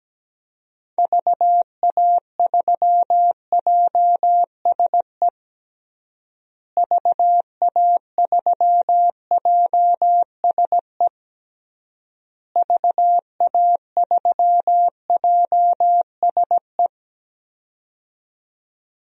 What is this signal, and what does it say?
Morse code, heard as a keyed tone, about 17 wpm, 700 Hz. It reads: VA3JSE VA3JSE VA3JSE
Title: DXCC-100 Call Signs - 6 Characters - Repeated 3x - 17wpm